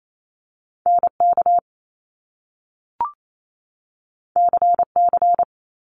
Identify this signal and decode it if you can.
Morse code, heard as a keyed tone, about 28 wpm, 700 Hz. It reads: NK CC
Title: Top Two Letter Combinations in Words 28wpm